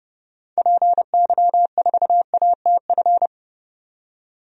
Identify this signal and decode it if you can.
Morse code, heard as a keyed tone, about 30 wpm, 700 Hz. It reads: PY4ATF
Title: DXCC-100 Call Signs - 6 Characters - 30wpm